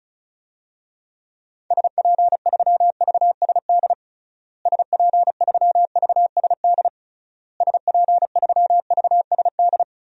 Morse code, heard as a keyed tone, about 34 wpm, 700 Hz. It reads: SP3VSD SP3VSD SP3VSD
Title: DXCC-100 Call Signs - 6 Characters - Repeated 3x - 35wpm